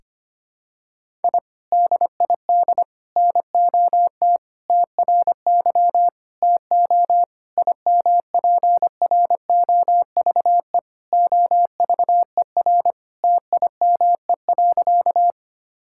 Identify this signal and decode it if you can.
Morse code, heard as a keyed tone, about 25 wpm, 700 Hz. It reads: I DID NOT TRY TO IMPROVE OVER TIME.
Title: Sentences from Top 1000 Words - Review Part 5 - 25wpm